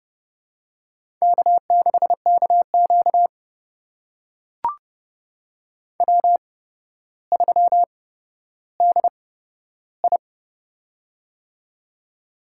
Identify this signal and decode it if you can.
Morse code, heard as a keyed tone, about 31 wpm, spaced as for 13 wpm, 700 Hz. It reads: K6KQ W3DI
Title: USA Call Signs - 8x Character Spacing - 30wpm